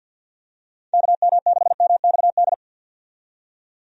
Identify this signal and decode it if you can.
Morse code, heard as a keyed tone, about 49 wpm, 700 Hz. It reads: KM6MXD
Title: USA Call Signs - 6 Characters - 50wpm